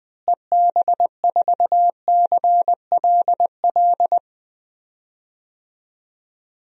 Morse code, heard as a keyed tone, about 20 wpm, 700 Hz. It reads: EB4CLL
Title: DXCC-100 Call Signs - 6 Characters - Repeated 3x - 20wpm